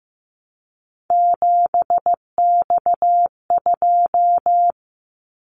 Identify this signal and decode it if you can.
Morse code, heard as a keyed tone, about 15 wpm, 700 Hz. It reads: 7X2
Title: Callsign Prefixes for Top 100 DXCC - 15wpm